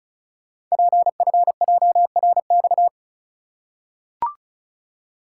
Morse code, heard as a keyed tone, about 36 wpm, 690 Hz. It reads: PFJRX
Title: Code Groups - 5 Letters - 35wpm